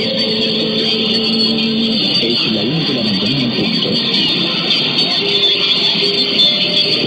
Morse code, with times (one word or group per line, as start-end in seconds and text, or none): none